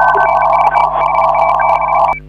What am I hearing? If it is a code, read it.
Morse code, heard as a keyed tone, about 37 wpm, 2300 Hz, with radio noise. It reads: TEST HA6T